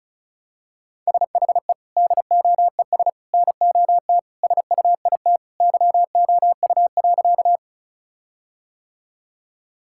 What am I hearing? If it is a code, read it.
Morse code, heard as a keyed tone, about 36 wpm, 700 Hz. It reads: SHE DOES NOT SUIT YOU.
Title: Sentences from Top 1600 Words 35wpm